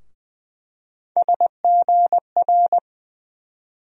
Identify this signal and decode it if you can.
Morse code, heard as a keyed tone, about 20 wpm, 700 Hz. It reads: SGR
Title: USA Call Sign Postfixes - 3 Letters 20wpm